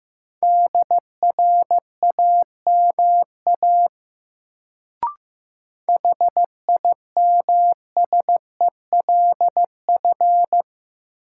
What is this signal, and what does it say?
Morse code, heard as a keyed tone, about 15 wpm, 700 Hz. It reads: DRAMA HIMSELF
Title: Sentences from Top 1500 Words - New Word Drill - 15wpm